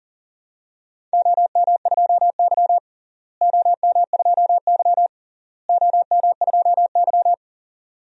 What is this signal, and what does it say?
Morse code, heard as a keyed tone, about 40 wpm, 700 Hz. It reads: OM2Y OM2Y OM2Y
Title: DXCC-100 Call Signs - 4 Characters - Repeated 3x - 40wpm